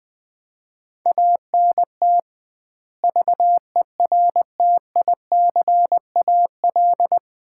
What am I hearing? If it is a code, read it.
Morse code, heard as a keyed tone, about 20 wpm, 700 Hz. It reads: ANT VERTICAL
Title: Common QSO Phrases - 2x Word-Spacing - 20wpm